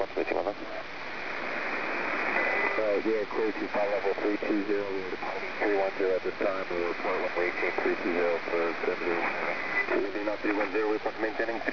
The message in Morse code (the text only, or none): none